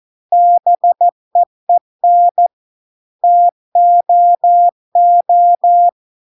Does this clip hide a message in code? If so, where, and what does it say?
Yes: Morse code from 0.3 to 5.9 s, heard as a keyed tone, about 14 wpm, 700 Hz. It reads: BEEN TOO